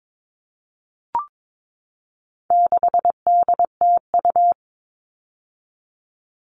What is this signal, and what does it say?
Morse code, heard as a keyed tone, about 22 wpm, 700 Hz. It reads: 6DTU
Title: All Cut Numbers - 4-Digit 22wpm